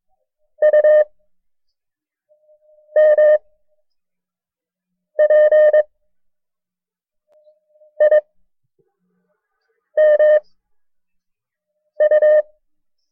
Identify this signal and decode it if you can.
Morse code, heard as a keyed tone, about 22 wpm, 610 Hz. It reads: U M P I M U
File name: FreeFiles / CW-practice / KB6MT-tape1-side2